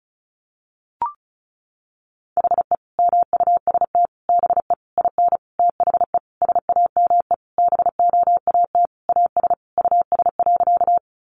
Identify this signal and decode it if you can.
Morse code, heard as a keyed tone, about 34 wpm, 700 Hz. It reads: HE MUST BE IN THE SAME BOAT AS US.
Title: Sentences from Top 500 Words 35wpm